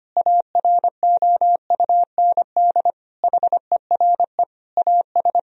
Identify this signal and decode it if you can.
Morse code, heard as a keyed tone, about 25 wpm, 700 Hz. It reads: AROUND HERE AS
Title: Sentences from Top 1000 Words - Review Part 3 - 25wpm